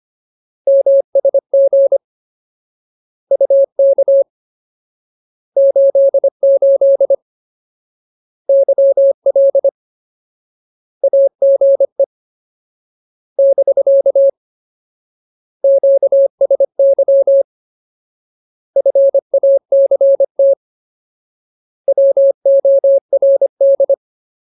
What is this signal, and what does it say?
Morse code, heard as a keyed tone, about 25 wpm, 550 Hz. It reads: MSG UK 88 YL AGE <BK> QSY FACT WORD
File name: Ultimate Word List / 25WPM Ultimate